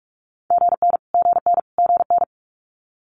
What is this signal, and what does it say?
Morse code, heard as a keyed tone, about 46 wpm, 700 Hz. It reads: GN GN GN
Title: Multi-letter Phonemes Repeated 3x 45wpm